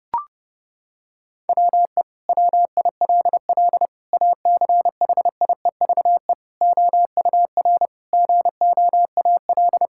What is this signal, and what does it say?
Morse code, heard as a keyed tone, about 32 wpm, 700 Hz. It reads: WE WILL ACHIEVE OUR GOAL
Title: Sentences from Top 1000 Words - Review Part 6 - 30wpm